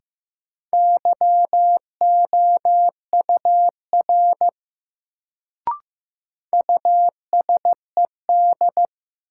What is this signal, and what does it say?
Morse code, heard as a keyed tone, about 16 wpm, 700 Hz. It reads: YOUR USED
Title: Sets of 2 Words - New Word Drill - 15wpm